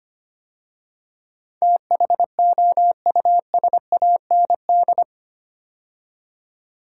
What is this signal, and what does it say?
Morse code, heard as a keyed tone, about 25 wpm, 700 Hz. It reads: THOUSAND
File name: 8 Letter Words - First 3 Letters Given - 25wpm